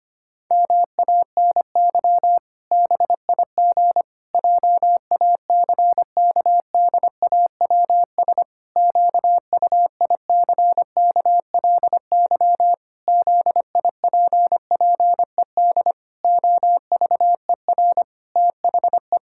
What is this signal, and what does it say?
Morse code, heard as a keyed tone, about 25 wpm, 700 Hz. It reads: MANY BIG JACKDAWS QUICKLY ZIPPED OVER THE
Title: Pangram Challenge v2 - 25wpm